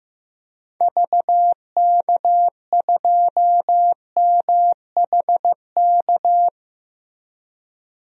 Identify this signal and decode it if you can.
Morse code, heard as a keyed tone, about 15 wpm, 700 Hz. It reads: VK2MHK
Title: DXCC-100 Call Signs - 6 Characters - Repeated 2x - 15wpm